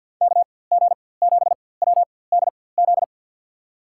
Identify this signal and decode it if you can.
Morse code, heard as a keyed tone, about 46 wpm, spaced as for 23 wpm, 700 Hz. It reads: KG7WDZ
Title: USA Call Signs - 6 Characters - 4x Character Spacing - 50wpm